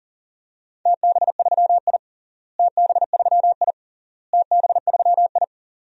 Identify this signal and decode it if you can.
Morse code, heard as a keyed tone, about 40 wpm, 700 Hz. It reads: TB3I TB3I TB3I
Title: DXCC-100 Call Signs - 4 Characters - Repeated 3x - 40wpm